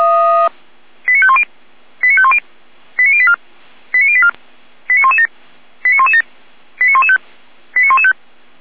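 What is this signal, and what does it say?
Five-tone selective call (ZVEI), heard as quick runs of five tones, tones of about 70 ms. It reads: 89510 89510 89084 89084 89117 89117 89115 89115
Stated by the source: ZVEI five-tone call 89120 received by SDR, 455 channel.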